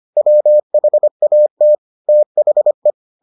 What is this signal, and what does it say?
Morse code, heard as a keyed tone, about 25 wpm, 600 Hz. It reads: WHAT THE